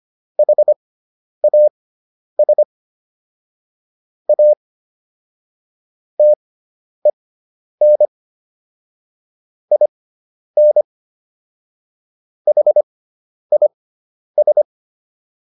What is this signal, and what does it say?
Morse code, heard as a keyed tone, about 25 wpm, spaced as for 10 wpm, 600 Hz. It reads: HAS A TEN IN HIS